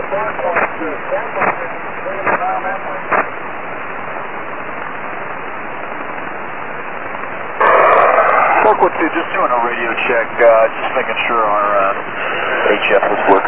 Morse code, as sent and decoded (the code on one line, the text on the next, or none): none